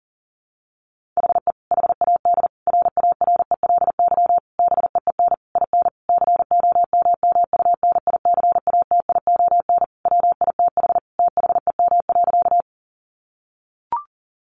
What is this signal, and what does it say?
Morse code, heard as a keyed tone, about 37 wpm, 700 Hz. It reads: HE HAD RARELY BEEN IN COMMUNICATION WITH THEM.
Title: Sentences from Top 1400 Words 40wpm